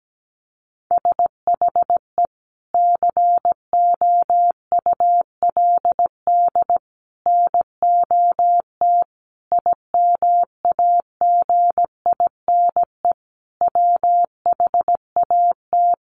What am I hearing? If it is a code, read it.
Morse code, heard as a keyed tone, about 17 wpm, 700 Hz. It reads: SHE COULD NOT IMAGINE WHAT
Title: Sentences from Top 1400 Words 17wpm